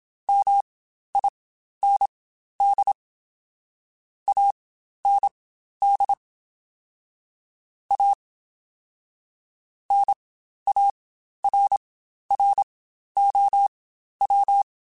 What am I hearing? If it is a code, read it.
Morse code, heard as a keyed tone, about 27 wpm, spaced as for 12 wpm, 800 Hz. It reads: MIND AND A NARROW
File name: Spaced / Chuckles Spaced-mp3s / MiddleAge sp @25